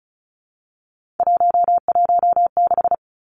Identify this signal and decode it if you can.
Morse code, heard as a keyed tone, about 34 wpm, 700 Hz. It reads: 116
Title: Numbers - 3-Digit 35wpm